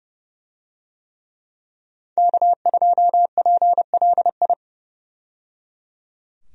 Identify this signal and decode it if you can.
Morse code, heard as a keyed tone, about 30 wpm, 700 Hz. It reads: K2PLI
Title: USA Call Signs - 5 Characters - 30wpm